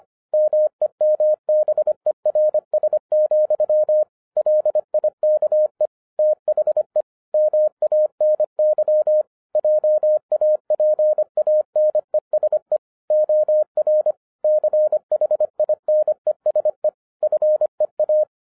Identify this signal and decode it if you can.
Morse code, heard as a keyed tone, about 25 wpm, 610 Hz. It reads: MEMBERS, LIKE THE MANY JAPANESE OR CHINESE FEA
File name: 2012-01-mp3 / n06-FIRST FISTS CONTACT-25wpm